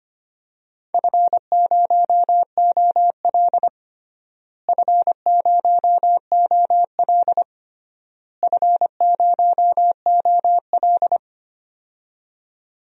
Morse code, heard as a keyed tone, about 25 wpm, 700 Hz. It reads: F0OL F0OL F0OL
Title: DXCC-100 Call Signs - 4 Characters - Repeated 3x - 25wpm